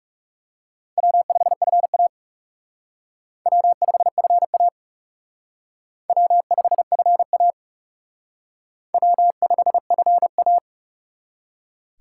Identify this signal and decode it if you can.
Morse code, heard as a keyed tone, about 37 wpm, 700 Hz. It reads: W5FA W5FA W5FA W5FA
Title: Speed Racing - USA 1x2 Call Signs 30wpm